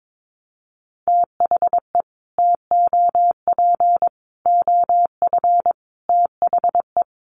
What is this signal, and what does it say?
Morse code, heard as a keyed tone, about 22 wpm, 700 Hz. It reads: THE TOP OF THE
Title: Sets of 5 Words Encore 22wpm